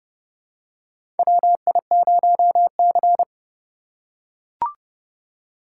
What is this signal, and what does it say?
Morse code, heard as a keyed tone, about 30 wpm, 700 Hz. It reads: WI0C
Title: USA Call Signs - 4 Characters - 4x Character Spacing - 30wpm